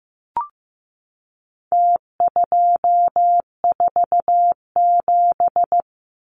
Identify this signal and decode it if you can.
Morse code, heard as a keyed tone, about 15 wpm, 700 Hz. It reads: T247
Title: Numbers - Contest Style - 4-Digit 15wpm